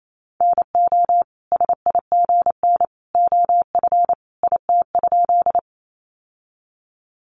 Morse code, heard as a keyed tone, about 28 wpm, 700 Hz. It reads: NO SIGN OF IT?
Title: Sentences from Top 1000 Words 28wpm